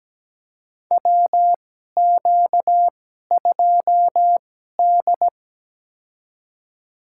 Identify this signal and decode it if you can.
Morse code, heard as a keyed tone, about 17 wpm, spaced as for 12 wpm, 700 Hz. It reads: WQ2D
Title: USA Call Signs - 2x Character Spacing - 17wpm